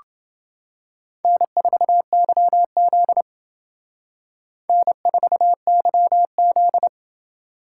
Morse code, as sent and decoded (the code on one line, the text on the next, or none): -. ....- -.-- --.. / -. ....- -.-- --..
N4YZ N4YZ